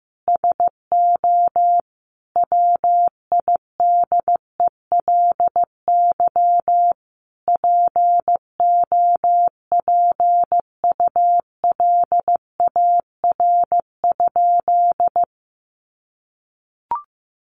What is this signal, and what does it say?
Morse code, heard as a keyed tone, about 15 wpm, 700 Hz. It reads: SO WIDELY POPULAR?
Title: Sentences from Top 1200 Words 15wpm